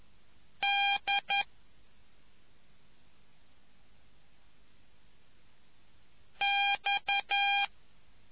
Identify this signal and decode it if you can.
Morse code, heard as a keyed tone, about 11 wpm, 2400 Hz. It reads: D X